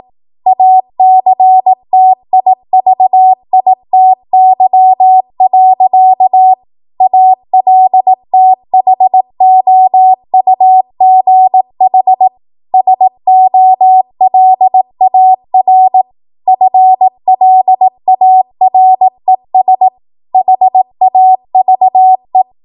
Morse code, heard as a keyed tone, about 18 wpm, 750 Hz. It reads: ACTIVITY. ALTHOUGH SOLAR FLARES HAVE
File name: Morse / 18WPM